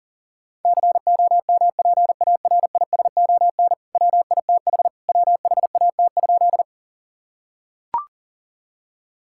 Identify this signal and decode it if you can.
Morse code, heard as a keyed tone, about 40 wpm, 700 Hz. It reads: COMPARISON WITH WHAT?
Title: Sentences from Top 1600 Words 40wpm